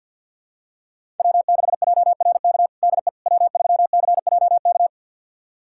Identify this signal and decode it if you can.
Morse code, heard as a keyed tone, about 51 wpm, 690 Hz. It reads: W6JRK DE W3CJK
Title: USA Callsign DE Callsign - 5 Characters - 50wpm